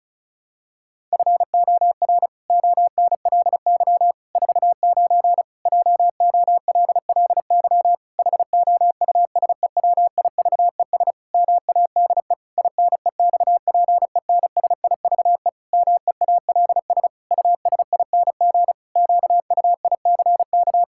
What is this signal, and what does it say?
Morse code, heard as a keyed tone, about 36 wpm, 700 Hz. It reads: FOR ONLY 49 JOLLY HOUSEWIVES MADE INEXPENSIVE MEALS USING QUICK
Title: Pangram Challenge v2 - 35wpm